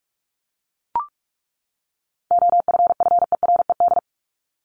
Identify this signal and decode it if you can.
Morse code, heard as a keyed tone, about 44 wpm, 700 Hz. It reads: OFFERED
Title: ED Suffix - Top 100 Words with Base Words 45wpm